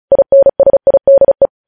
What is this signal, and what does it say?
Morse code, heard as a keyed tone, about 36 wpm, 550 Hz. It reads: INSIDE
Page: News Headlines in Morse Code at 35 WPM